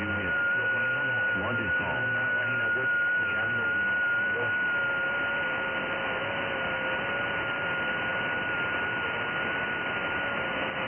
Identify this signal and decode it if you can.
FT8, with speech.
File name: IC-7100